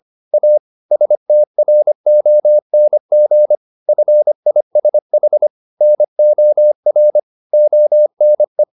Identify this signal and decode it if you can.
Morse code, heard as a keyed tone, about 25 wpm, 600 Hz. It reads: A STRONG FISH NOR ONE
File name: the old man and the sea part 1 25wpm